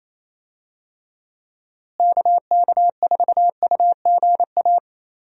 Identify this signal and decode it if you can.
Morse code, heard as a keyed tone, about 27 wpm, 700 Hz. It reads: KK4UGA
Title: USA Call Signs - 6 Characters - Repeated 3x - 28wpm